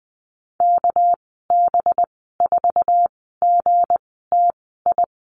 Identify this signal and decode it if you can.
Morse code, heard as a keyed tone, about 20 wpm, spaced as for 14 wpm, 700 Hz. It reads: KB4GTI
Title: 200 Lesson 32g - Letter k - Call Signs 20wpm